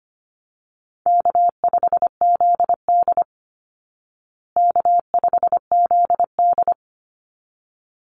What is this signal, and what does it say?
Morse code, heard as a keyed tone, about 25 wpm, 700 Hz. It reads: K5ZD K5ZD
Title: Easy Call Signs Repeated 2x - 25wpm